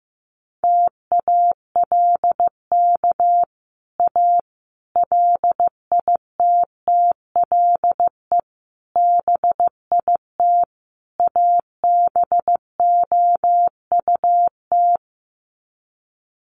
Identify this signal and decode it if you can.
Morse code, heard as a keyed tone, about 15 wpm, 700 Hz. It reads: TALK A LITTLE BIT ABOUT
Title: Sets of 5 Words Encore 15wpm